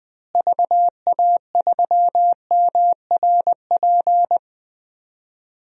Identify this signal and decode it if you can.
Morse code, heard as a keyed tone, about 21 wpm, 700 Hz. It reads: VA3MRP
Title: DXCC-100 Call Signs - 6 Characters - Repeated 2x - 20wpm